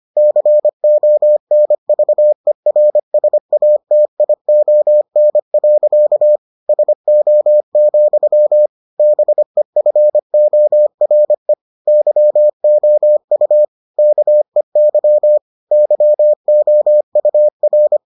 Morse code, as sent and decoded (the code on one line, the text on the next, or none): -.-. --- -. ...- . .-. ... .- - .. --- -. .-.-.- / ... --- --..-- / -... ..-. --- .-. / -.-- --- ..- / -.- . -.-- / -.-- --- ..- .-.
CONVERSATION. SO, BEFORE YOU KEY YOUR